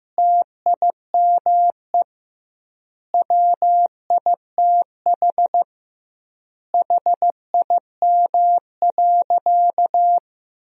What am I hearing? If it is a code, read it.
Morse code, heard as a keyed tone, about 15 wpm, 700 Hz. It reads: TIME WITH HIM.
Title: 2x Word-Spacing - Sentences from Top 300 Words - 15wpm